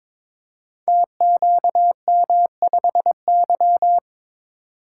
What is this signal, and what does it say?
Morse code, heard as a keyed tone, about 22 wpm, 700 Hz. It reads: TQM5Y
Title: Code Groups - 5 Letter:Numbers - 22wpm